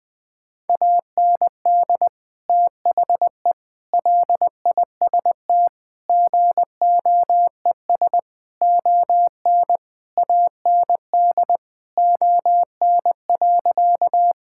Morse code, heard as a keyed tone, about 20 wpm, 700 Hz. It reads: AND THE LIST GOES ON AND ON.
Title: Sentences with 7-Word Phrases - 20wpm